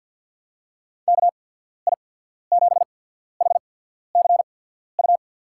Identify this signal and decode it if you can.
Morse code, heard as a keyed tone, about 52 wpm, spaced as for 14 wpm, 690 Hz. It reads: KI7HCU
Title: USA Call Signs - 6 Characters - 8x Character Spacing - 50wpm